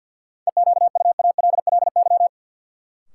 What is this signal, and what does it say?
Morse code, heard as a keyed tone, about 49 wpm, 700 Hz. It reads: EQUALLY
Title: Sentences from Top 1400 Words - New Word Drill Alt - 50wpm